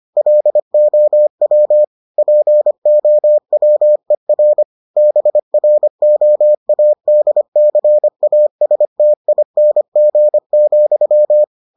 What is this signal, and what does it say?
Morse code, heard as a keyed tone, about 25 wpm, 600 Hz. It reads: LOW POWER BROADCASTING,